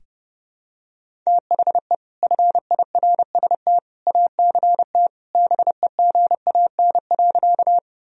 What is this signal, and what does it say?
Morse code, heard as a keyed tone, about 30 wpm, 700 Hz. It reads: THE FIRST ACT BEGAN.
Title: Sentences from Top 1000 Words - Review Part 1 - 30wpm